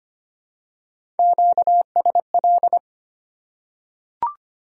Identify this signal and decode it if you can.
Morse code, heard as a keyed tone, about 25 wpm, 700 Hz. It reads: QSL
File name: Top 90 QSO Elements - 3x Character Spacing - 25wpm